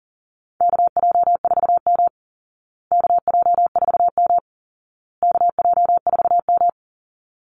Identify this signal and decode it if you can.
Morse code, heard as a keyed tone, about 37 wpm, 700 Hz. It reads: KJ4M KJ4M KJ4M
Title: Call Signs Repeated 2x - 40wpm